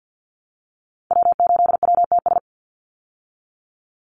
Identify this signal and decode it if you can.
Morse code, heard as a keyed tone, about 50 wpm, 700 Hz. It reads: W8WTS
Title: Call Signs Repeated 2x - 50wpm